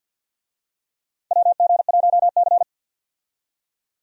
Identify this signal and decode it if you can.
Morse code, heard as a keyed tone, about 47 wpm, 700 Hz. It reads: WG1C
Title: USA Call Signs - 4 Characters - 50wpm